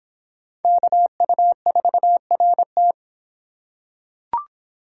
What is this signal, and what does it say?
Morse code, heard as a keyed tone, about 25 wpm, 700 Hz. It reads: KU4RT